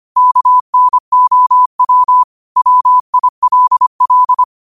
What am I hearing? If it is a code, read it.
Morse code, heard as a keyed tone, about 25 wpm, 1000 Hz. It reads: KNOW WILL
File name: qotdfiles / 25wpm